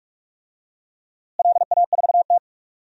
Morse code, heard as a keyed tone, about 44 wpm, 700 Hz. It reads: LA4T
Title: DXCC-100 Call Signs - 4 Characters - Repeated 3x - 45wpm